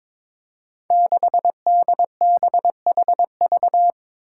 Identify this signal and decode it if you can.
Morse code, heard as a keyed tone, about 22 wpm, 700 Hz. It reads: 6DBHV
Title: Five Challenging Letter-Number Combinations 22wpm